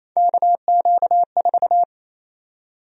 Morse code, heard as a keyed tone, about 28 wpm, 700 Hz. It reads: KQ4
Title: USA Call Sign Prefixes - 28wpm